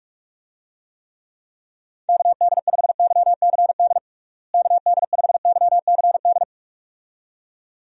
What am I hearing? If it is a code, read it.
Morse code, heard as a keyed tone, about 41 wpm, 700 Hz. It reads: KD5YCD KD5YCD
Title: USA Call Signs - 6 Characters - Repeated 2x - 45wpm